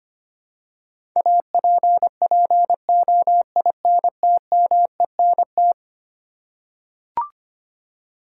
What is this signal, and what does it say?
Morse code, heard as a keyed tone, about 25 wpm, 700 Hz. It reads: APPOINTMENT
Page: morse-code-ninja-practice-25wpm